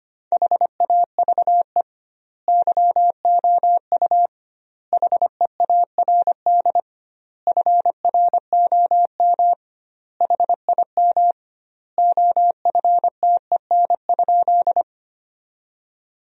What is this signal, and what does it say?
Morse code, heard as a keyed tone, about 25 wpm, 700 Hz. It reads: HAVE YOU HEARD FROM HIM OFTEN?
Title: 2x Word-Spacing - Sentences from Top 400 Words - 25wpm